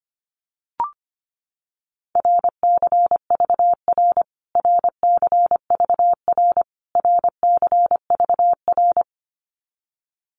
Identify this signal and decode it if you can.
Morse code, heard as a keyed tone, about 25 wpm, 700 Hz. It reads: RCVR RCVR RCVR
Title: Abbreviations Repeated 3x - 25wpm